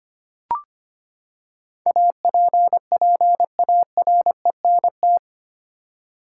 Morse code, heard as a keyed tone, about 25 wpm, 700 Hz. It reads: APPARENT
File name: Eight Letter Words - 25wpm